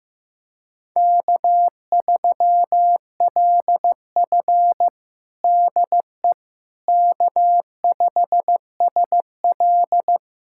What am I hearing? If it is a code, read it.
Morse code, heard as a keyed tone, about 15 wpm, 700 Hz. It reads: K3LF DE K5SL
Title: USA Callsign DE Callsign - 4 Characters - 15wpm